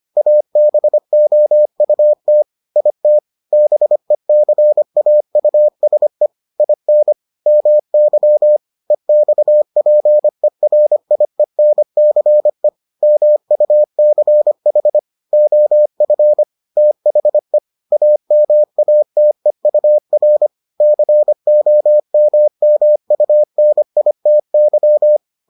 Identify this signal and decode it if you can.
Morse code, heard as a keyed tone, about 25 wpm, 600 Hz. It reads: ABOUT IT BECAUSE IN MY EXPERIENCE MUCH OF THE AMATEUR COMMUNITY